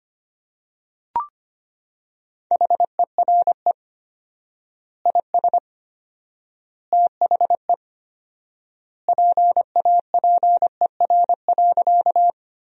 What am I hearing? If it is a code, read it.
Morse code, heard as a keyed tone, about 25 wpm, 700 Hz. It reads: HERE IS THE PAPER.
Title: 4x Word-Spacing - Sentences from Top 300 Words - 25wpm